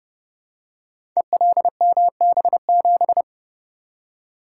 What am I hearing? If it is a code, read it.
Morse code, heard as a keyed tone, about 32 wpm, 700 Hz. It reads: ELMB7
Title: Code Groups - 5 Letter:Numbers - 30wpm